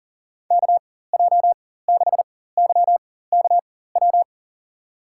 Morse code, heard as a keyed tone, about 40 wpm, spaced as for 19 wpm, 700 Hz. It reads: KJ6YKW
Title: USA Call Signs - 6 Characters - 4x Character Spacing - 40wpm